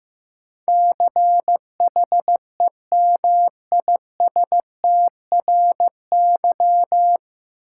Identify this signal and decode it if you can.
Morse code, heard as a keyed tone, about 15 wpm, 700 Hz. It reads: CHEMISTRY
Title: Sentences from Top 1000 Words - Review Part 6 - New Word Drill - 15wpm